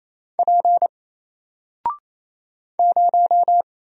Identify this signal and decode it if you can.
Morse code, heard as a keyed tone, about 28 wpm, 700 Hz. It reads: P 0
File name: Single-Letter-Number-28wpm